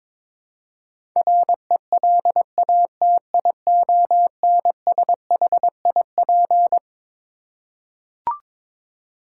Morse code, heard as a keyed tone, about 22 wpm, 700 Hz. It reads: RELATIONSHIP
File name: Twelve Letter Words - 22wpm